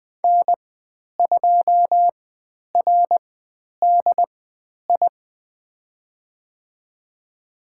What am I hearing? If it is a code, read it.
Morse code, heard as a keyed tone, about 20 wpm, spaced as for 10 wpm, 700 Hz. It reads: N2RDI
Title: 095 Lesson 16g - Letter c - Call Signs 20wpm